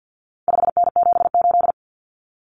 Morse code, heard as a keyed tone, about 50 wpm, 700 Hz. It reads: 5N78
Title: Common Cut Numbers - 4-Digit 50wpm